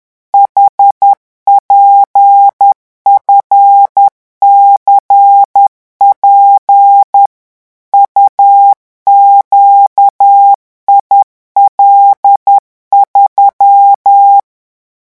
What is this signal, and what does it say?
Morse code, heard as a keyed tone, about 10 wpm, 800 Hz. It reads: HPFCP UQIL3
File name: morse mp3 / revision16-vitesse 10 mots